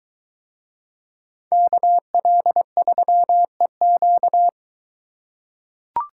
Morse code, heard as a keyed tone, about 22 wpm, 700 Hz. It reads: KL3EQ